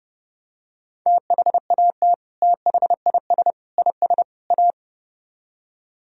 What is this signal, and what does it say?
Morse code, heard as a keyed tone, about 29 wpm, 700 Hz. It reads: THAT THIS IS A